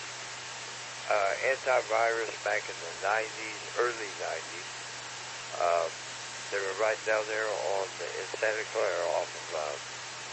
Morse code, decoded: none

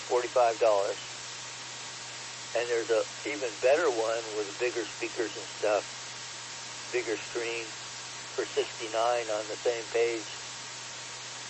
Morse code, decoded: none

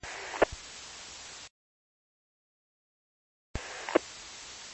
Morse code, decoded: none

